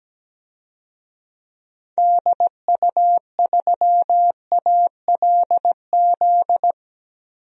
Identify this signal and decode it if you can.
Morse code, heard as a keyed tone, about 17 wpm, 700 Hz. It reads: DU3ALZ
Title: DXCC-100 Call Signs - 6 Characters - 17wpm